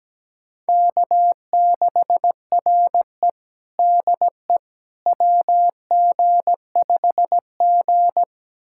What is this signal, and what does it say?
Morse code, heard as a keyed tone, about 17 wpm, 700 Hz. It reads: K6RE DE WG5G
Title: USA Callsign DE Callsign - 4 Characters - 17wpm